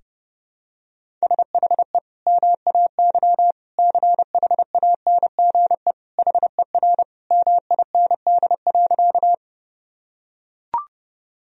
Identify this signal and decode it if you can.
Morse code, heard as a keyed tone, about 30 wpm, 700 Hz. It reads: SHE MAY CHANGE HER MIND.
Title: Sentences from Top 1000 Words - Review Part 1 - 30wpm